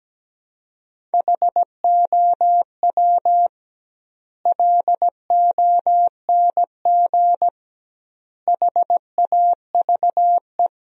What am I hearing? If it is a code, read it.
Morse code, heard as a keyed tone, about 17 wpm, 700 Hz. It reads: HOW LONG HAVE
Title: 2x Word-Spacing - Sentences from Top 200 Words - 17wpm